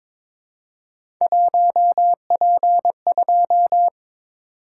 Morse code, heard as a keyed tone, about 22 wpm, 700 Hz. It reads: 1P2
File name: practice / 22wpm / Three Letter-Number Combinations - 22wpm